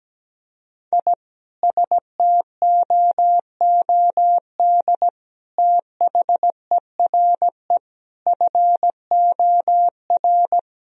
Morse code, heard as a keyed tone, about 17 wpm, 700 Hz. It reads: I STOOD THERE FOR
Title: Sentences from Top 1000 Words - Review Part 2 - 17wpm